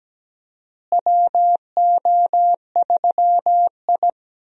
Morse code, heard as a keyed tone, about 17 wpm, 700 Hz. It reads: WO3I
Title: USA Call Signs - 4 Characters - Repeated 3x - 17wpm